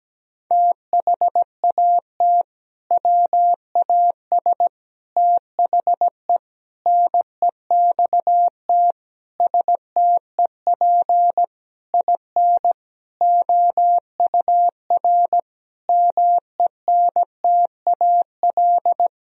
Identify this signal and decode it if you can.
Morse code, heard as a keyed tone, about 17 wpm, 700 Hz. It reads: THAT WAS THE NEXT STEP IN OUR MENTAL
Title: Sentences from Top 1000 Words - Review Part 4 - 17wpm